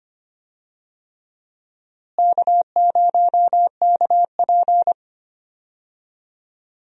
Morse code, heard as a keyed tone, about 25 wpm, 700 Hz. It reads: K0KP
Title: USA Call Signs - 4 Characters - Repeated 3x - 25wpm